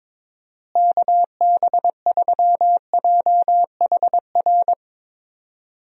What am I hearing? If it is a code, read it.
Morse code, heard as a keyed tone, about 22 wpm, 700 Hz. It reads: KB3JHR
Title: USA Call Signs - 6 Characters - Repeated 2x - 22wpm